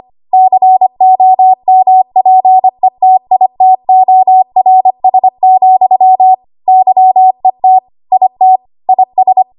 Morse code, heard as a keyed tone, about 25 wpm, 750 Hz. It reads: COMPETITORS, YET IT IS
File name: Archive / 25 WPM / 25WPM